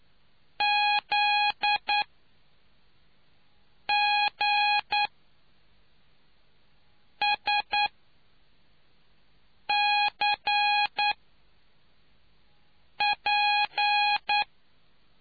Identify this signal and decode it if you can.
Morse code, heard as a keyed tone, about 9 wpm, spaced as for 4 wpm, 830 Hz. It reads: ZGSCP